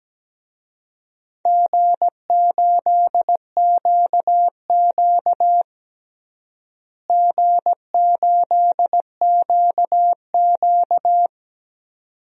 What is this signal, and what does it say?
Morse code, heard as a keyed tone, about 17 wpm, 700 Hz. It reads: G8QQ G8QQ
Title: DXCC-100 Call Signs - 4 Characters - Repeated 2x - 17wpm